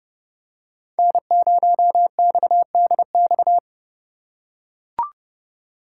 Morse code, heard as a keyed tone, about 31 wpm, 700 Hz. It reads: N0XDX